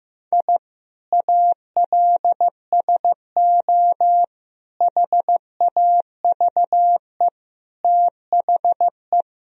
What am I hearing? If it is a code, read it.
Morse code, heard as a keyed tone, about 15 wpm, 700 Hz. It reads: I ALSO HAVE THE
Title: Sentences from Top 1000 Words - Review Part 1 - 15wpm